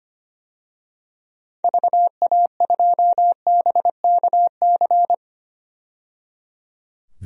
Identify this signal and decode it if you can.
Morse code, heard as a keyed tone, about 25 wpm, 700 Hz. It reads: VA2BKC
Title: DXCC-100 Call Signs - 6 Characters - 25wpm